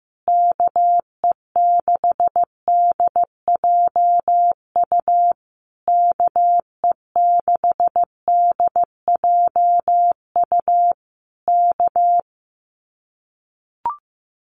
Morse code, heard as a keyed tone, about 15 wpm, 700 Hz. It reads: KE6DJU KE6DJU K
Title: Common QSO Phrases Encore 15wpm